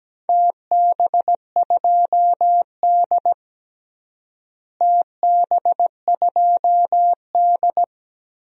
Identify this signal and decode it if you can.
Morse code, heard as a keyed tone, about 17 wpm, 700 Hz. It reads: TB2D TB2D
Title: DXCC-100 Call Signs - 4 Characters - Repeated 3x - 17wpm